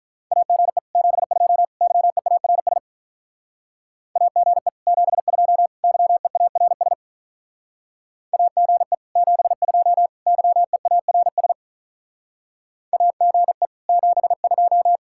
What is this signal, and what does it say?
Morse code, heard as a keyed tone, about 46 wpm, 700 Hz. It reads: AGE 72 YEARS AGE 72 YEARS AGE 72 YEARS AGE 72